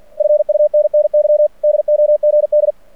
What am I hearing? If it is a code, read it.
Morse code, heard as a keyed tone, about 49 wpm, 600 Hz. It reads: FUNNY DOGD
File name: archived / CRSnet